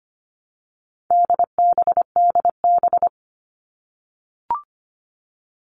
Morse code, heard as a keyed tone, about 25 wpm, 700 Hz. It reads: DBDB
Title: All Cut Numbers - 4-Digit 25wpm